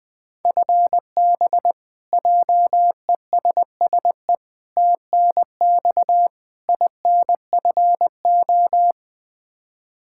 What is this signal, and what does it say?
Morse code, heard as a keyed tone, about 20 wpm, 700 Hz. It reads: FB JESSE TNX INFO
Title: Practice QSO - Exchange 3 - More Info - 20wpm